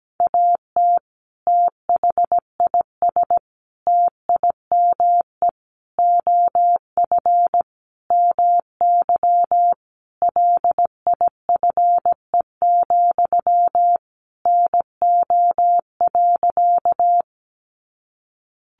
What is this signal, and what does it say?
Morse code, heard as a keyed tone, about 17 wpm, 700 Hz. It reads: AT THIS TIME OF MY LIFE, NO.
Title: Sentences from Top 300 Words 17wpm